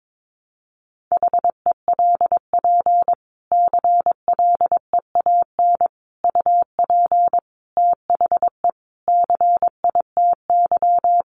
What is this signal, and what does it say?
Morse code, heard as a keyed tone, about 22 wpm, 700 Hz. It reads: HELP CLEAN UP THE CITY
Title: Sentences from Top 1100 Words 22wpm